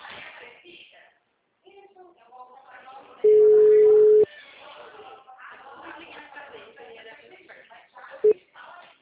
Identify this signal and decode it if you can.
Ringback tone (Europe, 425 Hz), with speech.
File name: out- -2003- - - .75549